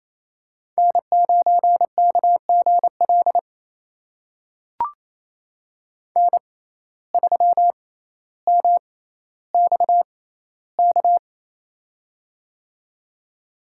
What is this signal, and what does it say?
Morse code, heard as a keyed tone, about 28 wpm, spaced as for 14 wpm, 700 Hz. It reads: N9KGL N3MXK
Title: USA Call Signs - 5 Characters - 6x Character Spacing - 28wpm